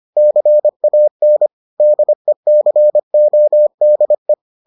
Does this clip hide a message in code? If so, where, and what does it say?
Yes: Morse code from 0.2 to 4.3 s, heard as a keyed tone, about 25 wpm, 600 Hz. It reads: CAN DECODE